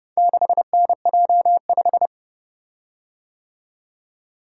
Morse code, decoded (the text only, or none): 6NJ5